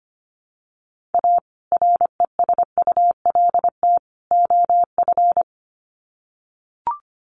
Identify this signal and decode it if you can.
Morse code, heard as a keyed tone, about 25 wpm, 700 Hz. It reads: A RESULT OF
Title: Sets of 3 Words 25wpm